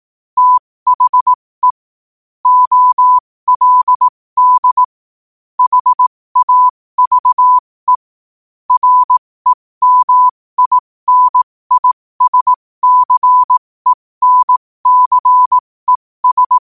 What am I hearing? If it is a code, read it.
Morse code, heard as a keyed tone, about 18 wpm, spaced as for 15 wpm, 1000 Hz. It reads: THE OLD HAVE REMINISCENCES
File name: qotdfiles / 15wpm